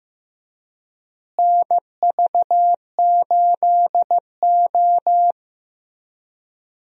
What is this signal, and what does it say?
Morse code, heard as a keyed tone, about 15 wpm, 700 Hz. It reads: NV8O
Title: USA Call Signs - 4 Characters - Repeated 3x - 15wpm